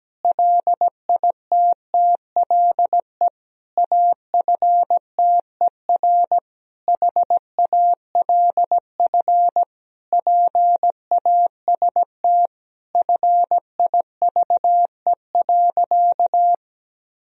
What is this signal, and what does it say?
Morse code, heard as a keyed tone, about 17 wpm, 700 Hz. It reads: LITTLE AFTER HALF PAST FIVE.
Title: Sentences from Top 1000 Words - Review Part 3 - 17wpm